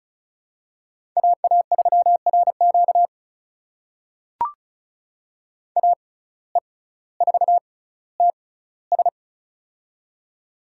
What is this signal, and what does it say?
Morse code, heard as a keyed tone, about 36 wpm, spaced as for 18 wpm, 700 Hz. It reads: AA3RQ AE4TS
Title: USA Call Signs - 5 Characters - 6x Character Spacing - 35wpm